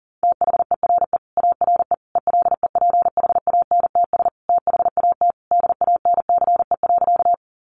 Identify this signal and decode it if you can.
Morse code, heard as a keyed tone, about 40 wpm, 700 Hz. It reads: THERE ARE ELEPHANTS THAT DANCE.
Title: Easy Phrases 40wpm